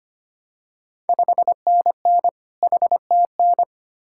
Morse code, heard as a keyed tone, about 25 wpm, 690 Hz. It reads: 5NN HTN
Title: Ontario QSO Party Exchange 25wpm